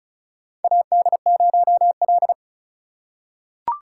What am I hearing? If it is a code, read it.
Morse code, heard as a keyed tone, about 34 wpm, 700 Hz. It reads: AD0L
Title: USA Call Signs - 4 Characters - 6x Character Spacing - 35wpm